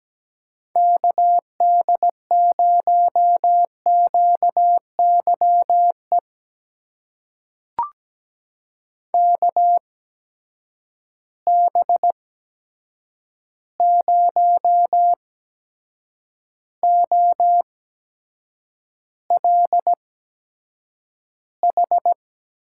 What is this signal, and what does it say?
Morse code, heard as a keyed tone, about 17 wpm, spaced as for 7 wpm, 700 Hz. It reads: KD0QYE KB0OLH
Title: USA Call Signs - 6 Characters - 8x Character Spacing - 17wpm